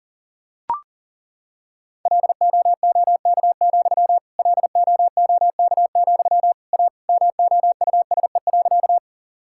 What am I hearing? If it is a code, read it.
Morse code, heard as a keyed tone, about 40 wpm, 700 Hz. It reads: LOOK, LOOK, A MOUSE.
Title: Sentences from Top 1000 Words - Review Part 5 - 40wpm